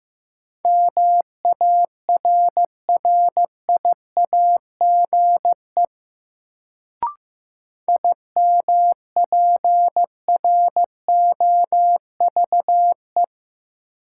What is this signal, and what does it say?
Morse code, heard as a keyed tone, about 15 wpm, 700 Hz. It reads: MARRIAGE IMPROVE
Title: Sentences from Top 800 Words - New Word Drill - 15wpm